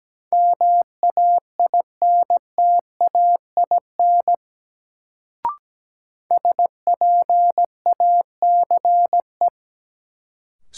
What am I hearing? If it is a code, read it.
Morse code, heard as a keyed tone, about 17 wpm, 700 Hz. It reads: MAINTAIN SPACE
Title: Sentences from Top 1000 Words - Review Part 5 - New Word Drill - 17wpm